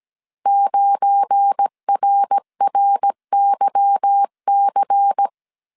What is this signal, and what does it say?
Morse code, heard as a keyed tone, about 17 wpm, 800 Hz. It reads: 9RRYC
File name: CW / DET / 1 9 1500C S A 16wpm 2828